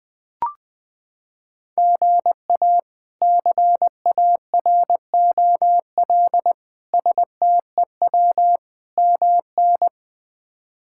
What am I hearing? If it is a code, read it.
Morse code, heard as a keyed tone, about 20 wpm, 700 Hz. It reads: GA CAROL STEW MN
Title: K1USN SST QSO Practice - 20wpm